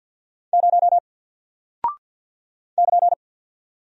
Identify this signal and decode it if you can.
Morse code, heard as a keyed tone, about 48 wpm, 700 Hz. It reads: 0 <KN>
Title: Single Character (Letter:Number:Punctuation:Prosign) 50wpm